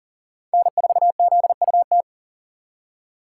Morse code, heard as a keyed tone, about 40 wpm, 700 Hz. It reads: N4ZUT
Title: USA Call Signs - 5 Characters - 2x Character Spacing - 40wpm